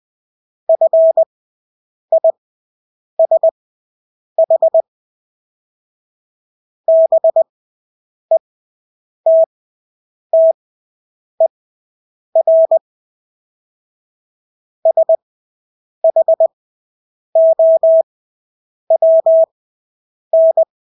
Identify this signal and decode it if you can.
Morse code, heard as a keyed tone, about 20 wpm, spaced as for 8 wpm, 650 Hz. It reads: FISH BETTER SHOWN
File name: Engelske ord 20/8 wpm